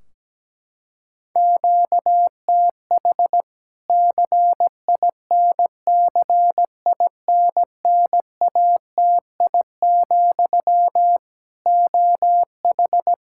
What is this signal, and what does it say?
Morse code, heard as a keyed tone, about 17 wpm, 700 Hz. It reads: QTH CINCINNATI, OH
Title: K1USN SST QSO Practice 17wpm